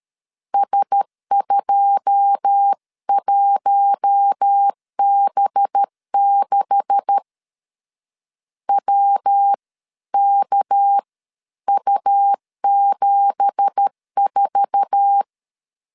Morse code, heard as a keyed tone, about 13 wpm, spaced as for 11 wpm, 800 Hz. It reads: S21B6 WKU74
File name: CW / DET / 1 9 1500C S B 12wpm 3819